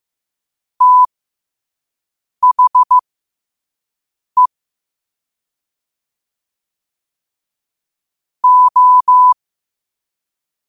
Morse code, heard as a keyed tone, about 15 wpm, spaced as for 5 wpm, 1000 Hz. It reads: THE O
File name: qotdfiles / 5wpm